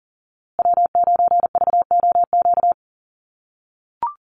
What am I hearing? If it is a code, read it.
Morse code, heard as a keyed tone, about 38 wpm, 700 Hz. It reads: W9VOQ
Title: USA Call Signs Repeated 3x 40wpm